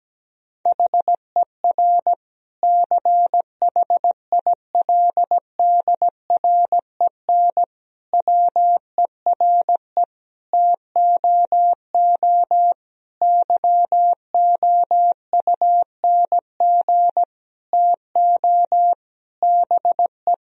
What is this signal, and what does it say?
Morse code, heard as a keyed tone, about 17 wpm, 700 Hz. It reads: HER CHILDREN WERE TOO YOUNG TO BE
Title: Sentences from Top 1000 Words - Review Part 4 - 17wpm